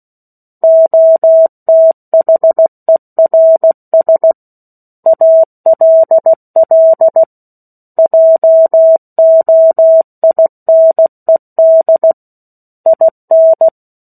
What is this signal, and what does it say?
Morse code, heard as a keyed tone, about 16 wpm, 650 Hz. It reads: OTHERS ALL JOINED IN
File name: 03-Chapter-03 16 WPM